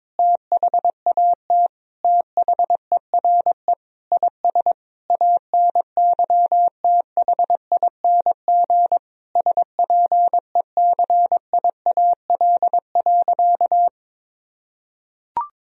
Morse code, heard as a keyed tone, about 22 wpm, 700 Hz. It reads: THAT THERE IS ANYTHING SPECIAL.
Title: Sentences from Top 1000 Words - Review Part 3 - 22wpm